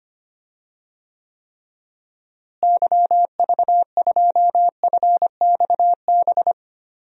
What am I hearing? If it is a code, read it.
Morse code, heard as a keyed tone, about 25 wpm, 700 Hz. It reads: YV2FXB
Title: DXCC-100 Call Signs - 6 Characters - Repeated 3x - 25wpm